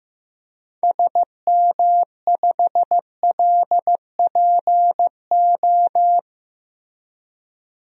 Morse code, heard as a keyed tone, about 15 wpm, 700 Hz. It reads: SM5LPO
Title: DXCC-100 Call Signs - 6 Characters - 15wpm